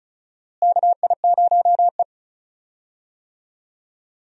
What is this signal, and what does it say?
Morse code, heard as a keyed tone, about 36 wpm, 700 Hz. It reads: KI0E
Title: USA Call Signs - 4 Characters - Repeated 3x - 35wpm